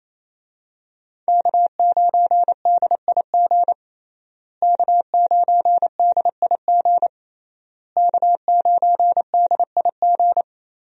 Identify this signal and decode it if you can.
Morse code, heard as a keyed tone, about 27 wpm, 700 Hz. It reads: K9DIG K9DIG K9DIG
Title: USA Call Signs - 5 Characters - Repeated 3x - 28wpm